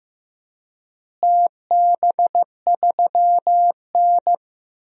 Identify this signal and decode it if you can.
Morse code, heard as a keyed tone, about 15 wpm, 700 Hz. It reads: TB3N
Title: DXCC-100 Call Signs - 4 Characters - Repeated 2x - 15wpm